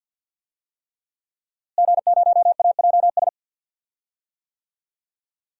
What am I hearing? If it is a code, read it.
Morse code, heard as a keyed tone, about 51 wpm, 700 Hz. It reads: G0AJS
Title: DXCC-100 Call Signs - 5 Characters - Repeated 3x - 50wpm